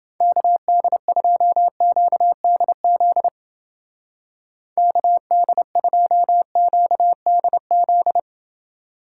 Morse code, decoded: KD2QDZ KD2QDZ

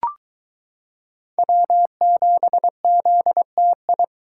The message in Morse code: W7ZTI

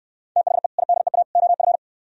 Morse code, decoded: THE IDEA OF

